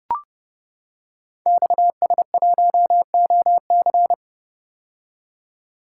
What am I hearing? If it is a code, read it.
Morse code, heard as a keyed tone, about 30 wpm, 700 Hz. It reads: XS1OC
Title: Code Groups - 5 Letter:Numbers - 30wpm